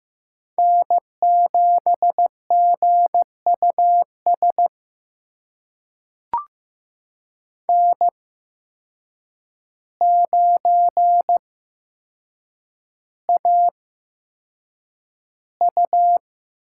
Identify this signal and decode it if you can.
Morse code, heard as a keyed tone, about 15 wpm, spaced as for 7 wpm, 700 Hz. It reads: N7GUS N9AU
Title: USA Call Signs - 5 Characters - 8x Character Spacing - 15wpm